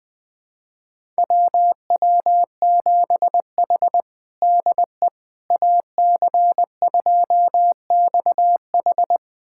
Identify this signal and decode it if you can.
Morse code, heard as a keyed tone, about 20 wpm, 700 Hz. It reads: WW7H DE AC2XH
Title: USA Callsign DE Callsign 20wpm